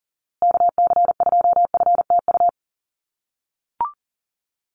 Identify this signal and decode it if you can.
Morse code, heard as a keyed tone, about 40 wpm, 700 Hz. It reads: KC2FTU
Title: USA Call Signs Repeated 2x 40wpm